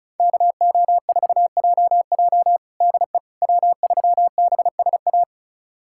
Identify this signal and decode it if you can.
Morse code, heard as a keyed tone, about 34 wpm, 700 Hz. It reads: KO4JJ DE W3BSA
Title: USA Callsign DE Callsign - 5 Characters - 35wpm